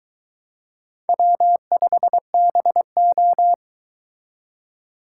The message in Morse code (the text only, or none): W5BO